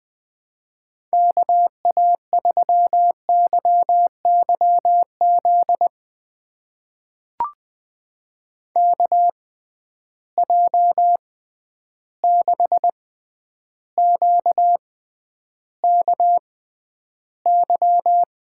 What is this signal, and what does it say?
Morse code, heard as a keyed tone, about 20 wpm, spaced as for 10 wpm, 700 Hz. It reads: KA3YYZ KJ6QKY
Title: USA Call Signs - 6 Characters - 6x Character Spacing - 20wpm